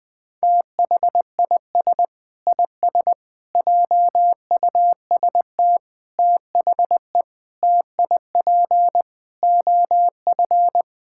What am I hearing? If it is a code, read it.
Morse code, heard as a keyed tone, about 20 wpm, 700 Hz. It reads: THIS IS JUST THE TIP OF
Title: Sentences with 8-Word Phrases - 20wpm